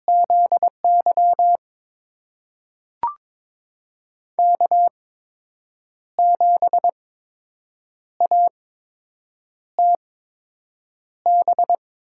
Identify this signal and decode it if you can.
Morse code, heard as a keyed tone, about 21 wpm, spaced as for 7 wpm, 700 Hz. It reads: ZY K7ATB
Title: USA Call Signs - 5 Characters - 8x Character Spacing - 22wpm